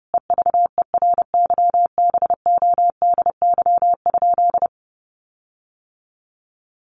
Morse code, heard as a keyed tone, about 30 wpm, 700 Hz. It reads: EVERYBODY?
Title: Sentences from Top 1400 Words 30wpm